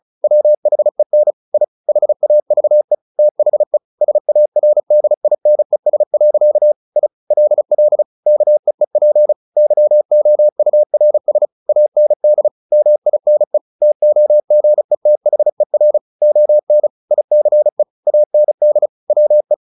Morse code, heard as a keyed tone, about 34 wpm, 600 Hz. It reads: WHEN I HAVE THE SARDINES. I LL KEEP YOURS AND MINE TOGETHER ON ICE AND WE
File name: the old man and the sea / the old man and the sea 1 35wpm